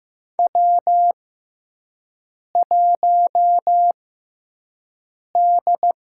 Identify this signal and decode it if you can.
Morse code, heard as a keyed tone, about 15 wpm, spaced as for 5 wpm, 700 Hz. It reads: W1D
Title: USA Call Signs - 4 Characters - 6x Character Spacing - 15wpm